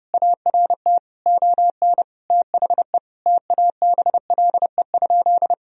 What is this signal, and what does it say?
Morse code, heard as a keyed tone, about 30 wpm, 700 Hz. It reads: ART ON THE TABLE?
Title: Sentences from Top 1000 Words - Review Part 3 - 30wpm